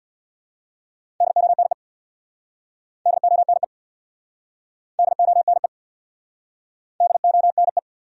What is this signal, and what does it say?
Morse code, heard as a keyed tone, about 66 wpm, 700 Hz. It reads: DONE DONE DONE DONE